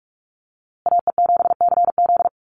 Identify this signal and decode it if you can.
Morse code, heard as a keyed tone, about 44 wpm, 700 Hz. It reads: AE7CZ
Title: USA Call Signs Repeated 3x 45wpm